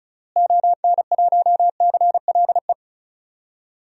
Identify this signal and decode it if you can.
Morse code, heard as a keyed tone, about 34 wpm, 700 Hz. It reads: ON1CLE
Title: DXCC-100 Call Signs - 6 Characters - Repeated 2x - 35wpm